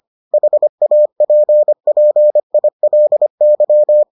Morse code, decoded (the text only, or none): HAPPILY